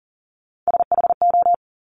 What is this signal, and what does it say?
Morse code, heard as a keyed tone, about 42 wpm, 700 Hz. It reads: SHO
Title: Top Three Letter Combinations in Words 40wpm